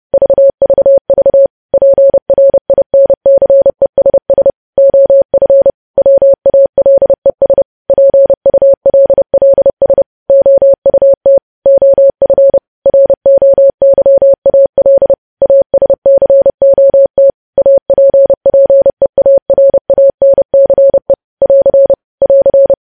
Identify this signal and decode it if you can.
Morse code, heard as a keyed tone, about 30 wpm, 550 Hz. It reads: VVV PRINCESS OF WALES PULLS OUT OF ROYAL ASCOT APPEARANCE <AR> <AR>
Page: news headlines in morse code at 30 wpm